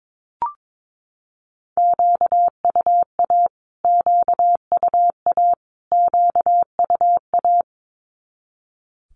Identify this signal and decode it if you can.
Morse code, heard as a keyed tone, about 22 wpm, 700 Hz. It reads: QUA QUA QUA
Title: Q-Codes Repeated 3x 22wpm